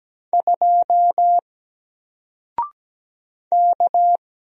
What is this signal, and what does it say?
Morse code, heard as a keyed tone, about 17 wpm, 700 Hz. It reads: 2 K
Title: Single Letter-Number 17wpm